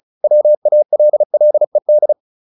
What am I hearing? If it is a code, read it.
Morse code, heard as a keyed tone, about 34 wpm, 600 Hz. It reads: WALLED